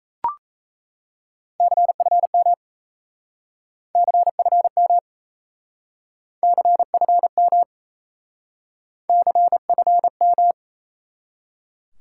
Morse code, heard as a keyed tone, about 37 wpm, 700 Hz. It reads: CFM CFM CFM CFM